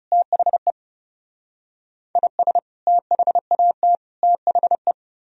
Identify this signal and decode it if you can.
Morse code, heard as a keyed tone, about 40 wpm, 700 Hz. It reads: THE IS THAT THE